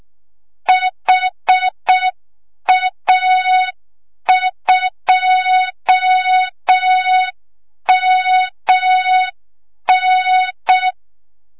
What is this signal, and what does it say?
Morse code, heard as a keyed tone, about 6 wpm, 3700 Hz, with noise. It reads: HA2MN
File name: Morze-HA2MN / hang / 25 GYAKORLAT